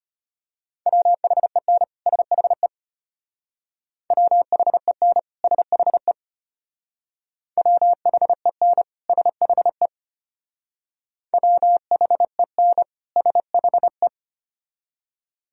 Morse code, heard as a keyed tone, about 33 wpm, 700 Hz. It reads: WHEN SHE WHEN SHE WHEN SHE WHEN SHE